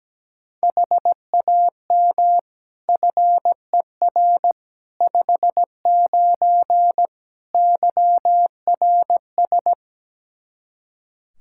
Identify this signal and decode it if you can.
Morse code, heard as a keyed tone, about 17 wpm, 700 Hz. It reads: HAM FER 59 YRS
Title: Practice QSO - Exchange 3 - More Info - 17wpm